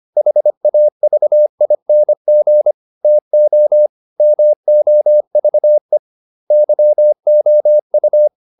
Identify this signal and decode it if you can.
Morse code, heard as a keyed tone, about 25 wpm, 600 Hz. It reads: HAVING TO MOVE YOU